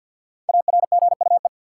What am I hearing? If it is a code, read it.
Morse code, heard as a keyed tone, about 53 wpm, 700 Hz. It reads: ARGUE
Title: Five Letter Words - 50wpm